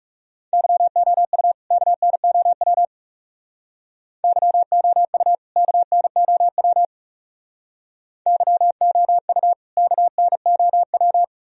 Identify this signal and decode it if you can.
Morse code, heard as a keyed tone, about 36 wpm, 700 Hz. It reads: YOU KNOW YOU KNOW YOU KNOW
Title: Speed-Racing - Sets of 2 Words 30wpm